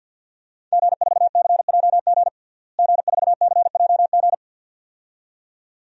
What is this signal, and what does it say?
Morse code, heard as a keyed tone, about 50 wpm, 700 Hz. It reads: G4CJG G4CJG
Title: DXCC-100 Call Signs - 5 Characters - Repeated 2x - 50wpm